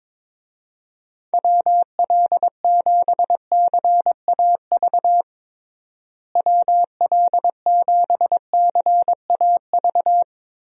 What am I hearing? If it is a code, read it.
Morse code, heard as a keyed tone, about 21 wpm, 700 Hz. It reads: WL7CAV WL7CAV
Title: USA Call Signs - 6 Characters - Repeated 3x - 22wpm